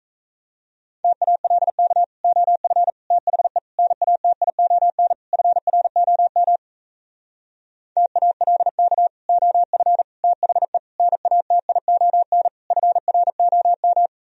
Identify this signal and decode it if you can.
Morse code, heard as a keyed tone, about 37 wpm, 700 Hz. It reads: TALK OF THE NATION FROM TALK OF THE NATION FROM